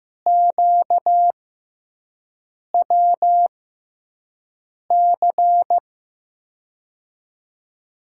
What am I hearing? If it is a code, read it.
Morse code, heard as a keyed tone, about 15 wpm, spaced as for 5 wpm, 700 Hz. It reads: QWC
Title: USA Call Signs - 6 Characters - 6x Character Spacing - 15wpm